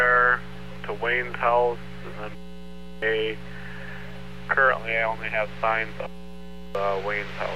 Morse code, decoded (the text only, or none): none